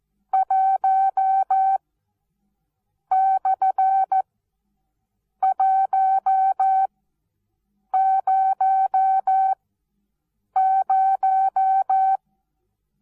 Morse code, heard as a keyed tone, about 14 wpm, spaced as for 6 wpm, 770 Hz. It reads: 1/100
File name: Onlinelehrgang / morsen / DJ4UF Morsen19